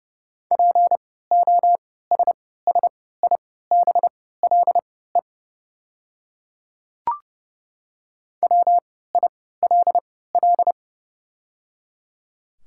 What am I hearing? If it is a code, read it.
Morse code, heard as a keyed tone, about 30 wpm, spaced as for 17 wpm, 690 Hz. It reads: POSSIBLE WILL
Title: 3x Character Spacing - Top 500 Words - 30wpm